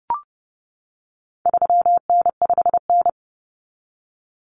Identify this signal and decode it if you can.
Morse code, heard as a keyed tone, about 30 wpm, 700 Hz. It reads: 3N5N
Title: Common Cut Numbers - 4-Digit 30wpm